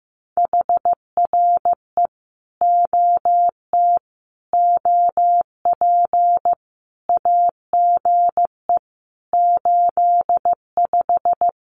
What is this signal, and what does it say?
Morse code, heard as a keyed tone, about 15 wpm, 700 Hz. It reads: HRE OT OP AGE 85
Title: Common QSO Phrases 15wpm